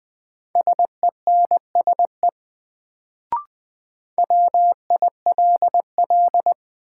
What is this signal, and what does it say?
Morse code, heard as a keyed tone, about 20 wpm, 700 Hz. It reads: SENSE WILL
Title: Sets of 2 Words - New Word Drill - 20wpm